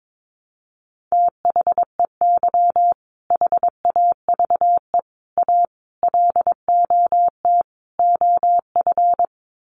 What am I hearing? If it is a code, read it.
Morse code, heard as a keyed tone, about 22 wpm, 700 Hz. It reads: THEY HAVE A LOT OF
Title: Sets of 5 Words Encore2 22wpm